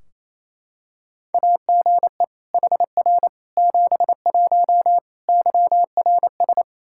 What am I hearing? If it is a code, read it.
Morse code, heard as a keyed tone, about 28 wpm, 700 Hz. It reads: AGE HR 71 YRS
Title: Practice QSO - Exchange 3 - More Info - 28wpm